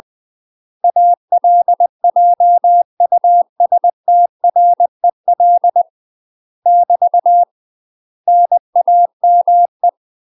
Morse code, heard as a keyed tone, about 20 wpm, 700 Hz. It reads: ALJUSTREL <BT> NAME
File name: UK MASTER QSOs TRAINING / 20WPM / UK MASTER QSOs 0038